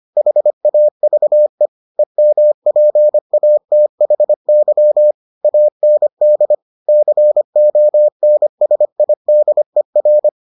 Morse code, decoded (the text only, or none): HAVE EMPATHY AND CONSIDER